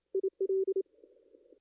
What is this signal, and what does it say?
Morse code, heard as a keyed tone, about 27 wpm, 380 Hz. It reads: IL
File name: contests / ss-cw-k5tr-op-n5rz -2300